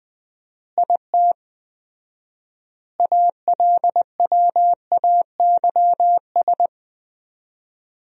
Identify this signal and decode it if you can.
Morse code, heard as a keyed tone, about 21 wpm, 700 Hz. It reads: IT ALWAYS
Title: 4x Word-Spacing - Sentences from Top 300 Words - 20wpm